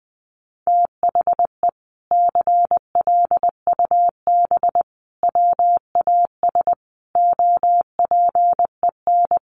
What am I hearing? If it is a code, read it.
Morse code, heard as a keyed tone, about 20 wpm, 700 Hz. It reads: THE CLUB WAS OPEN
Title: Sentences from Top 1200 Words 20wpm